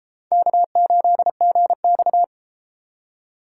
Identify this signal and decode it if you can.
Morse code, heard as a keyed tone, about 32 wpm, 700 Hz. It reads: K8GX